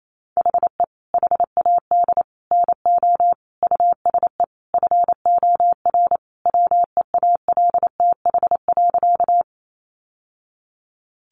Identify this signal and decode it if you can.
Morse code, heard as a keyed tone, about 28 wpm, 700 Hz. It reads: HE HAD NO USE FOR WEALTH.
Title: Sentences from Top 1200 Words 28wpm